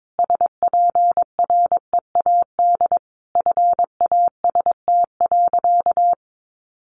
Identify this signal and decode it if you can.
Morse code, heard as a keyed tone, about 22 wpm, 700 Hz. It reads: SPREAD FAST.
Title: Sentences from Top 1200 Words 22wpm